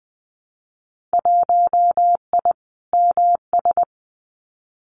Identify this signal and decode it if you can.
Morse code, heard as a keyed tone, about 20 wpm, 700 Hz. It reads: 1I MS
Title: Winter Field Day Exchanges Encore 20wpm